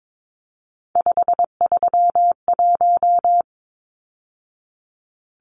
Morse code, heard as a keyed tone, about 22 wpm, 700 Hz. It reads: 531
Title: Numbers - 3-Digit 22wpm